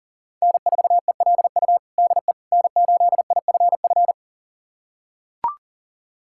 Morse code, heard as a keyed tone, about 40 wpm, 700 Hz. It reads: N4ELU DE N8IFF
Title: USA Callsign DE Callsign - 5 Characters - 40wpm